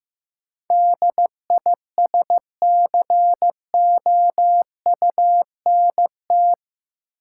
Morse code, heard as a keyed tone, about 15 wpm, 700 Hz. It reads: DISCOUNT
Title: Sentences from Top 1200 Words - New Word Drill Alt - 15wpm